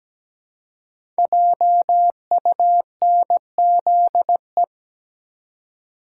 Morse code, heard as a keyed tone, about 17 wpm, 700 Hz. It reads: JUNZE
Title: Code Groups - 5 Letters - 17wpm